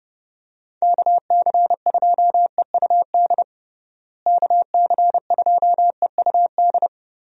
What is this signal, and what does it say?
Morse code, heard as a keyed tone, about 30 wpm, 700 Hz. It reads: KC2EUD KC2EUD
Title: USA Call Signs - 6 Characters - Repeated 2x - 30wpm